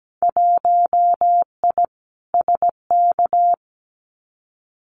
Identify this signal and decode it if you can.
Morse code, heard as a keyed tone, about 17 wpm, 700 Hz. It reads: 1I SK
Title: Winter Field Day Exchanges Encore 17wpm